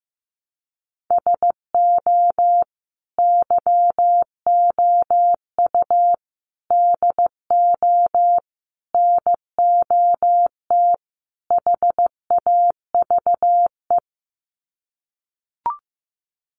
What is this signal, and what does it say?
Morse code, heard as a keyed tone, about 15 wpm, 700 Hz. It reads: SO YOU DO NOT HAVE
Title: Sets of 5 Words Encore 15wpm